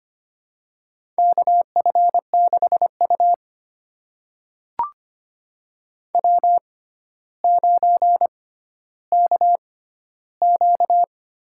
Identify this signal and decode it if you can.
Morse code, heard as a keyed tone, about 25 wpm, spaced as for 13 wpm, 700 Hz. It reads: KF6U W9KQ
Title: USA Call Signs - 4 Characters - 6x Character Spacing - 25wpm